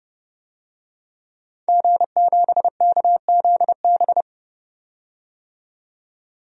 Morse code, heard as a keyed tone, about 30 wpm, 700 Hz. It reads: G7KZB
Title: DXCC-100 Call Signs - 5 Characters - 30wpm